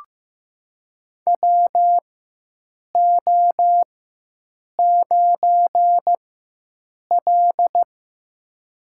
Morse code, heard as a keyed tone, about 15 wpm, spaced as for 7 wpm, 700 Hz. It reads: WO9L
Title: USA Call Signs - 4 Characters - 4x Character Spacing - 15wpm